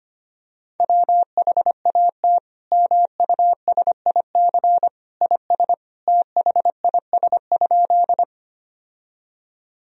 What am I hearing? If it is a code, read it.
Morse code, heard as a keyed tone, about 25 wpm, 700 Hz. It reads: WHAT MUSIC IS THIS?
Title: Sentences from Top 1000 Words - Review Part 1 - 25wpm